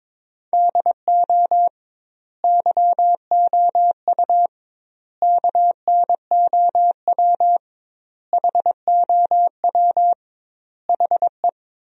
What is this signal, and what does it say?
Morse code, heard as a keyed tone, about 22 wpm, 700 Hz. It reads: DO YOU KNOW HOW HE
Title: 2x Word-Spacing - Sentences from Top 300 Words - 22wpm